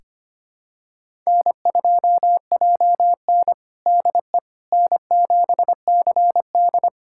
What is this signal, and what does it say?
Morse code, heard as a keyed tone, about 25 wpm, 700 Hz. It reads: N2JN DE N7CD
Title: USA Callsign DE Callsign - 4 Characters - 25wpm